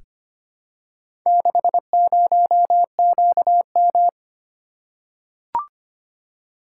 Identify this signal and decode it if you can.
Morse code, heard as a keyed tone, about 25 wpm, 700 Hz. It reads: 60QM
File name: practice / 25wpm / Four Letter-Number Combinations - 25wpm